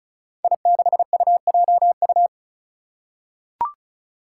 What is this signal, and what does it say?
Morse code, heard as a keyed tone, about 36 wpm, 700 Hz. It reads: I6UJU
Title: Five Challenging Letter-Number Combinations 35wpm